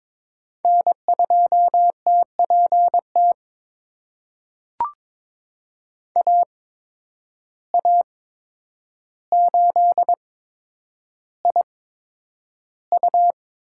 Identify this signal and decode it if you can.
Morse code, heard as a keyed tone, about 21 wpm, spaced as for 9 wpm, 700 Hz. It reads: N2TPT AA8IU
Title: USA Call Signs - 5 Characters - 8x Character Spacing - 22wpm